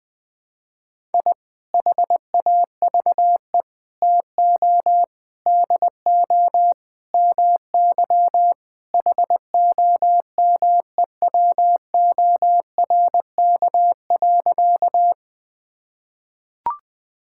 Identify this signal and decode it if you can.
Morse code, heard as a keyed tone, about 21 wpm, 700 Hz. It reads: I HAVE TO DO MY HOMEWORK.
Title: Sentences with 3-Word Phrases - Spoken Beginning - 20wpm